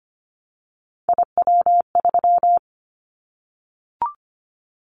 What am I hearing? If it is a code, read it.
Morse code, heard as a keyed tone, about 25 wpm, 700 Hz. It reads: IW3
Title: Callsign Prefixes for Top 25 DXCC - 25wpm